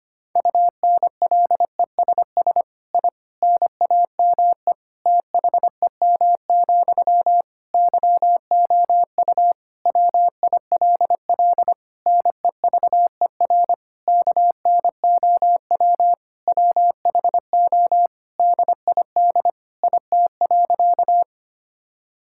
Morse code, heard as a keyed tone, about 25 wpm, 700 Hz. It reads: UNLESS I NAME THEM, YOU WILL NEVER KNOW WHO DID IT.
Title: Sentences from Top 1000 Words - Review Part 6 - 25wpm